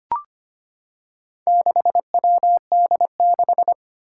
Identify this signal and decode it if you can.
Morse code, heard as a keyed tone, about 25 wpm, 700 Hz. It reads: 6WD6
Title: Four Challenging Letter-Number Combinations 25wpm